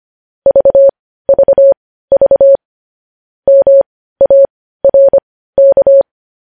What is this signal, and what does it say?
Morse code, heard as a keyed tone, about 25 wpm, spaced as for 15 wpm, 550 Hz. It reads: VVV MARK
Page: news headlines in morse code at 15 wpm